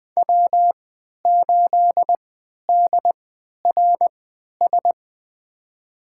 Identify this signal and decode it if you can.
Morse code, heard as a keyed tone, about 21 wpm, spaced as for 12 wpm, 700 Hz. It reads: W8DRS
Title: USA Call Signs - 5 Characters - 3x Character Spacing - 20wpm